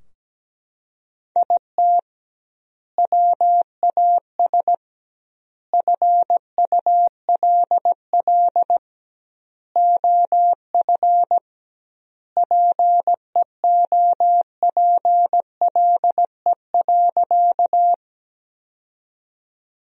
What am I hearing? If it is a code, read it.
Morse code, heard as a keyed tone, about 17 wpm, 700 Hz. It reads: IT WAS FULL OF PEOPLE.
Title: 2x Word-Spacing - Sentences from Top 500 Words - 17wpm